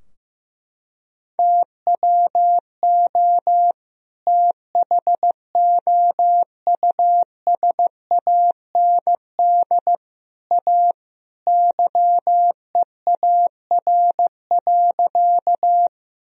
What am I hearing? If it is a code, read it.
Morse code, heard as a keyed tone, about 15 wpm, 700 Hz. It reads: TWO THOUSAND A YEAR.
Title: Sentences from Top 1000 Words - Review Part 3 - 15wpm